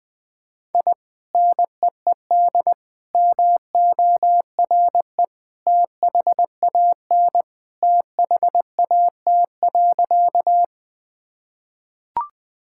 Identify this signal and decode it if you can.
Morse code, heard as a keyed tone, about 21 wpm, 700 Hz. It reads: I NEED MORE THAN THAT.
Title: Sentences with 2-Word Phrases - Spoken Beginning - 20wpm